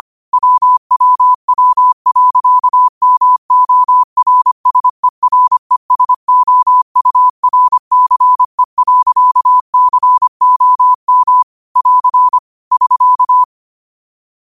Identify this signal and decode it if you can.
Morse code, heard as a keyed tone, about 25 wpm, 1000 Hz. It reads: WWW.MORSERESOURCE.COM <AR> <SK>
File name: qotdfiles / 25wpm